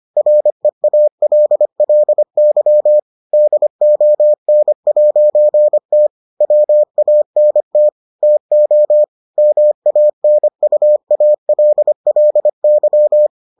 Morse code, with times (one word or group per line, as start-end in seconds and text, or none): 0.2-3.0 REALLY
3.3-6.1 DON'T
6.4-7.9 WANT
8.2-9.0 TO
9.4-13.3 MANUALLY